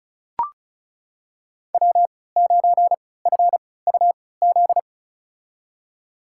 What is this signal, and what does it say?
Morse code, heard as a keyed tone, about 36 wpm, spaced as for 20 wpm, 700 Hz. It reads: W9FUZ
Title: USA Call Signs - 5 Characters - 3x Character Spacing - 35wpm